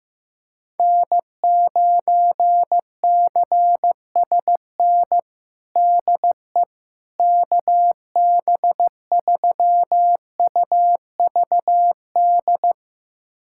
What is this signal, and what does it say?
Morse code, heard as a keyed tone, about 15 wpm, 700 Hz. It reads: N9CSN DE KB3UVD
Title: USA Callsign DE Callsign 15wpm